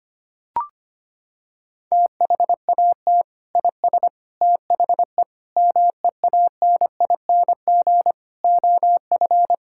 Morse code, heard as a keyed tone, about 25 wpm, 700 Hz. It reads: THAT IS THE MEANING OF
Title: Sentences from Top 1600 Words 25wpm